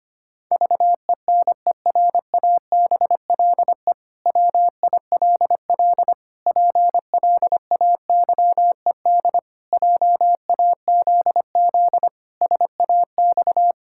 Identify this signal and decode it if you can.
Morse code, heard as a keyed tone, about 25 wpm, 700 Hz. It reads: VENERABLE WILL PLAYED JAZZ SAX
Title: Pangram Challenge v2 - 25wpm